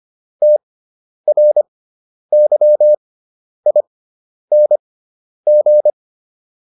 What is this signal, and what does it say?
Morse code, heard as a keyed tone, about 25 wpm, spaced as for 10 wpm, 600 Hz. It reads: TRYING